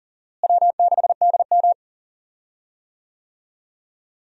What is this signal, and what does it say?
Morse code, heard as a keyed tone, about 39 wpm, 700 Hz. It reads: W6DM
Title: USA Call Signs - 4 Characters - Repeated 3x - 40wpm